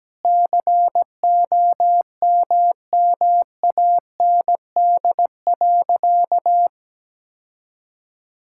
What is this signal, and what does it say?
Morse code, heard as a keyed tone, about 17 wpm, 700 Hz. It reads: COMMAND.
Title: Sentences from Top 1600 Words 17wpm